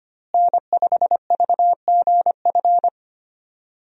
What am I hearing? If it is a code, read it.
Morse code, heard as a keyed tone, about 25 wpm, 700 Hz. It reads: N5VGF
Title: USA Call Signs - 5 Characters - Repeated 3x - 25wpm